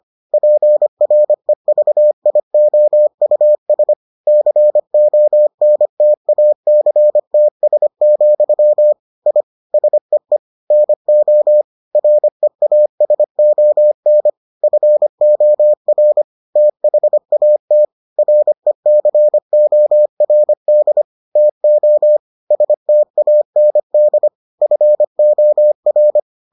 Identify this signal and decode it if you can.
Morse code, heard as a keyed tone, about 25 wpm, 600 Hz. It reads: PREVIOUS CONTACTS, I SEE NO REASON FOR THAT RECORD TO STAND FOR